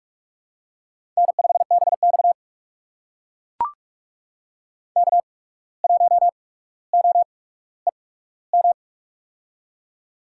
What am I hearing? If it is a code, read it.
Morse code, heard as a keyed tone, about 44 wpm, spaced as for 18 wpm, 690 Hz. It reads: N5BX K1OEM
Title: USA Call Signs - 8x Character Spacing - 45wpm